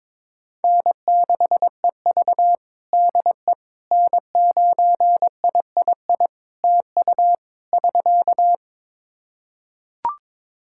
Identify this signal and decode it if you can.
Morse code, heard as a keyed tone, about 22 wpm, 700 Hz. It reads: N6EV DE N9III TU <SK>
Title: Practice QSO - Exchange 4 - Ending - 22wpm